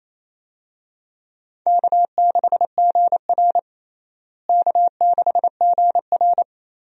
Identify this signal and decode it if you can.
Morse code, heard as a keyed tone, about 28 wpm, 700 Hz. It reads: K6GR K6GR
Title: USA Call Signs - 4 Characters - Repeated 3x - 28wpm